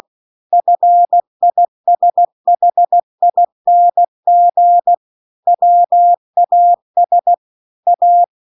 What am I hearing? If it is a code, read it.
Morse code, heard as a keyed tone, about 16 wpm, 700 Hz. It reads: FISHING WAS A